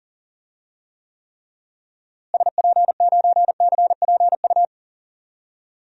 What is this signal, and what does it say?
Morse code, heard as a keyed tone, about 38 wpm, 700 Hz. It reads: SP9CPU
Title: DXCC-100 Call Signs - 6 Characters - Repeated 3x - 40wpm